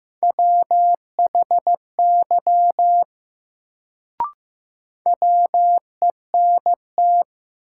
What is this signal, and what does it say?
Morse code, heard as a keyed tone, about 15 wpm, 700 Hz. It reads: WHY WENT
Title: Sentences from Top 200 Words - New Word Drill - 15wpm